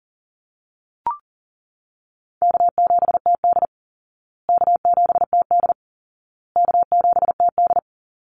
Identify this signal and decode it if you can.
Morse code, heard as a keyed tone, about 37 wpm, 700 Hz. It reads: K7TD K7TD K7TD
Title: Easy Call Signs Repeated 3x - 40wpm